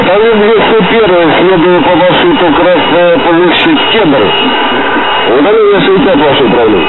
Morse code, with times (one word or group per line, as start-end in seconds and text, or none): none